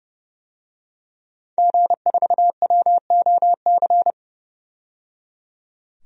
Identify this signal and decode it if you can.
Morse code, heard as a keyed tone, about 30 wpm, 700 Hz. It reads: G4WOC
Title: DXCC-100 Call Signs - 5 Characters - 30wpm